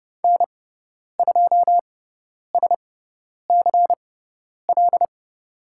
Morse code, heard as a keyed tone, about 30 wpm, spaced as for 10 wpm, 700 Hz. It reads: N2SCL
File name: Lesson 16g - Letter c - Call Signs 30wpm